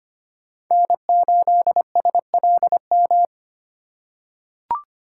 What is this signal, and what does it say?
Morse code, heard as a keyed tone, about 25 wpm, 700 Hz. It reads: N8SLM
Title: USA Call Signs - 3x Character Spacing - 25wpm